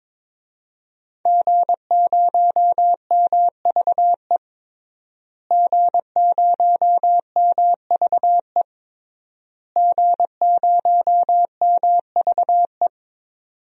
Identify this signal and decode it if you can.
Morse code, heard as a keyed tone, about 22 wpm, 700 Hz. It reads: G0MVE G0MVE G0MVE
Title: DXCC-100 Call Signs - 5 Characters - Repeated 3x - 22wpm